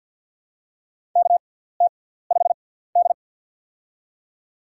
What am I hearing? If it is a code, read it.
Morse code, heard as a keyed tone, about 53 wpm, spaced as for 17 wpm, 700 Hz. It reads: KT5D
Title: USA Call Signs - 4 Characters - 6x Character Spacing - 50wpm